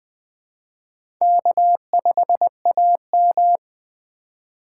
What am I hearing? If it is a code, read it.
Morse code, heard as a keyed tone, about 20 wpm, 700 Hz. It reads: K5AM
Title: K1USN SST QSO Practice - 20wpm